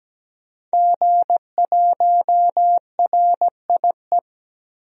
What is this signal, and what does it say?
Morse code, heard as a keyed tone, about 17 wpm, 700 Hz. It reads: G1RIE
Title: DXCC-100 Call Signs - 5 Characters - Repeated 3x - 17wpm